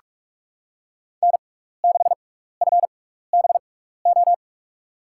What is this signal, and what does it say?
Morse code, heard as a keyed tone, about 46 wpm, spaced as for 16 wpm, 700 Hz. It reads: N6FBO
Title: USA Call Signs - 6x Character Spacing - 45wpm